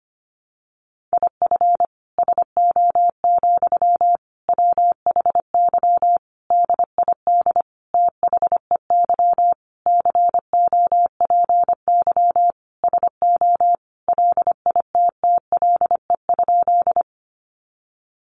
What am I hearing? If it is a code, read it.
Morse code, heard as a keyed tone, about 25 wpm, 700 Hz. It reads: IF SO, WHY DID THEY COPY SO LITTLE?
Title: Sentences from Top 1400 Words 25wpm